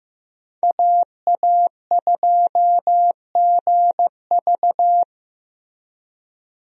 Morse code, heard as a keyed tone, about 15 wpm, 700 Hz. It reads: AA2GV
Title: USA Call Signs - 5 Characters - 15wpm